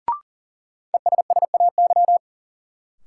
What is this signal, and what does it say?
Morse code, heard as a keyed tone, about 40 wpm, 700 Hz. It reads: ESSAY
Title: Sentences from Top 1500 Words - New Word Drill - 40wpm